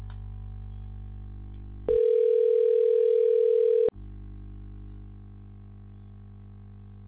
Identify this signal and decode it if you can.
Ringback tone (North America), 440 + 480 Hz.